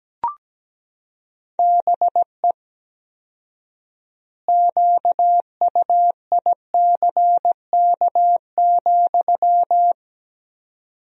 Morse code, heard as a keyed tone, about 17 wpm, 700 Hz. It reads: BE QUICK,
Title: 4x Word-Spacing - Sentences from Top 500 Words - 17wpm